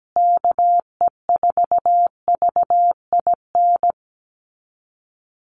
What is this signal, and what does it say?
Morse code, heard as a keyed tone, about 17 wpm, 700 Hz. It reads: KE4VIN
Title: USA Call Signs Repeated 3x 17wpm